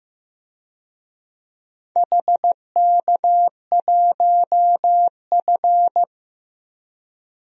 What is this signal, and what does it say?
Morse code, heard as a keyed tone, about 15 wpm, 700 Hz. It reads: HK1F
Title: DXCC-100 Call Signs - 4 Characters - 15wpm